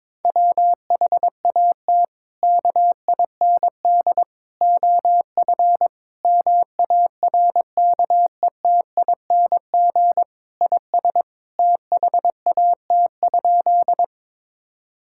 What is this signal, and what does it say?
Morse code, heard as a keyed tone, about 22 wpm, 700 Hz. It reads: WHAT KIND OF MARKETING IS THAT?
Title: Sentences from Top 1000 Words - Review Part 5 - 22wpm